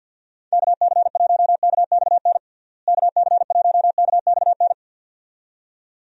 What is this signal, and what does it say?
Morse code, heard as a keyed tone, about 52 wpm, 700 Hz. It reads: KC1KXN KC1KXN
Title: USA Call Signs - 6 Characters - Repeated 2x - 50wpm